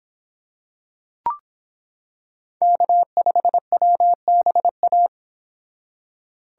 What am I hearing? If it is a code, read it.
Morse code, heard as a keyed tone, about 26 wpm, 700 Hz. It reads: K5WBA